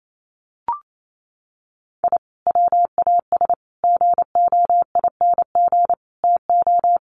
Morse code, heard as a keyed tone, about 28 wpm, 700 Hz. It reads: I WAS GOING TO
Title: Sets of 5 Words 28wpm